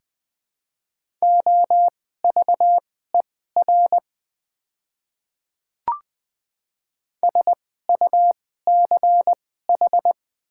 Morse code, heard as a keyed tone, about 21 wpm, spaced as for 15 wpm, 700 Hz. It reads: OVER SUCH
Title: 2x Character Spacing - Top 300 Words - 20wpm